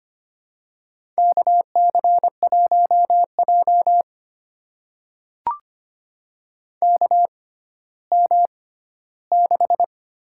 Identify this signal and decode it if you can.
Morse code, heard as a keyed tone, about 25 wpm, spaced as for 14 wpm, 700 Hz. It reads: KC1J KM6
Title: USA Call Signs - 6x Character Spacing - 25wpm